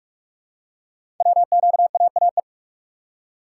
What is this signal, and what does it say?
Morse code, heard as a keyed tone, about 44 wpm, 700 Hz. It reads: WQAAE
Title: Code Groups - 5 Letter:Numbers - 45wpm